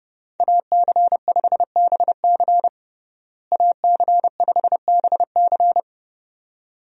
Morse code, heard as a keyed tone, about 30 wpm, 700 Hz. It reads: AC5BC AC5BC
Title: USA Call Signs - 5 Characters - Repeated 2x - 30wpm